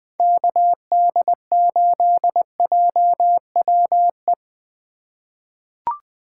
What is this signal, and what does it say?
Morse code, heard as a keyed tone, about 20 wpm, 700 Hz. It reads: KD8JWE
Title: USA Call Signs - 2x Character Spacing - 20wpm